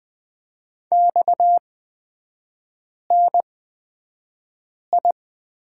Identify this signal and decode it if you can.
Morse code, heard as a keyed tone, about 20 wpm, spaced as for 5 wpm, 700 Hz. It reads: XNI
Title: 229 Lesson 37e - Letter x - Three Characters 20wpm